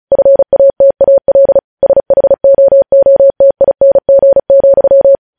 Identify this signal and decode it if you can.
Morse code, heard as a keyed tone, about 36 wpm, 550 Hz. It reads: FATAL SHOOTING,
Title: News Headlines at 35